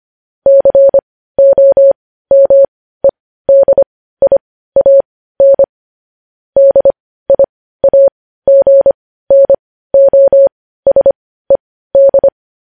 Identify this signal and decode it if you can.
Morse code, heard as a keyed tone, about 25 wpm, spaced as for 15 wpm, 550 Hz. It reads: COMEDIAN DIAGNOSED
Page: News Headlines in Morse Code at 15 WPM